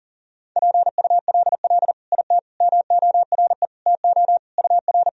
Morse code, heard as a keyed tone, about 40 wpm, 700 Hz. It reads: PULL IT MORE TO UR
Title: Sentences from Top 400 Words and Abbreviations 40wpm